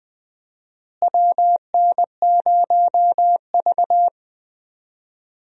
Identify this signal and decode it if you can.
Morse code, heard as a keyed tone, about 20 wpm, 700 Hz. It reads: WN0V
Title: USA Call Signs - 4 Characters - Repeated 2x - 20wpm